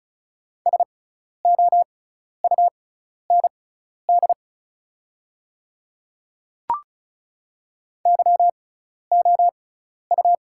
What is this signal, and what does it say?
Morse code, heard as a keyed tone, about 36 wpm, spaced as for 12 wpm, 700 Hz. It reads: SOUND YOU
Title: 6x Character Spacing - Top 100 Words - 35wpm